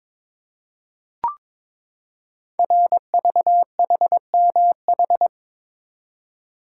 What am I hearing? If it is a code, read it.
Morse code, heard as a keyed tone, about 22 wpm, 700 Hz. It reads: RVHMH
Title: Code Groups - 5 Letters - 22wpm